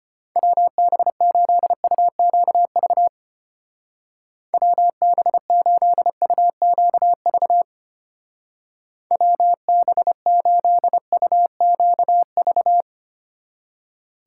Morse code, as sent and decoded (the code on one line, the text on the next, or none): .-- -... ---.. ..- --.- ...- / .-- -... ---.. ..- --.- ...- / .-- -... ---.. ..- --.- ...-
WB8UQV WB8UQV WB8UQV